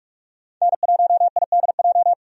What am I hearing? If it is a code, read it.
Morse code, heard as a keyed tone, about 45 wpm, 700 Hz. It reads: N1IDJ
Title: USA Call Signs - 5 Characters - 45wpm